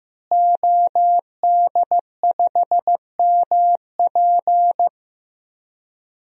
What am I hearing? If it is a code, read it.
Morse code, heard as a keyed tone, about 15 wpm, 700 Hz. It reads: OD5MP
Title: DXCC-100 Call Signs - 5 Characters - Repeated 2x - 15wpm